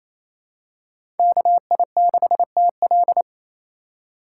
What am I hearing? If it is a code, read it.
Morse code, heard as a keyed tone, about 28 wpm, 700 Hz. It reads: KI6TL
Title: USA Call Signs - 2x Character Spacing - 28wpm